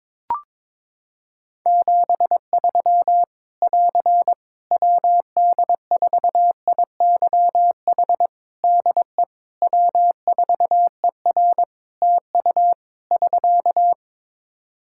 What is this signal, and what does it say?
Morse code, heard as a keyed tone, about 22 wpm, 700 Hz. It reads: 73 <AR> WD4IYH DE W4ER TU <SK>
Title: Practice QSO - Exchange 4 - Ending - 22wpm